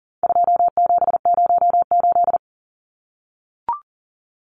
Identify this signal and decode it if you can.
Morse code, heard as a keyed tone, about 37 wpm, 700 Hz. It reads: 2708
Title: Numbers - 4-Digit 40wpm